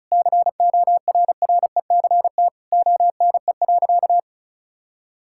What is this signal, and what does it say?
Morse code, heard as a keyed tone, about 36 wpm, 700 Hz. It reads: CORRECT ONE.
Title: Sentences from Top 1000 Words - Review Part 6 - 35wpm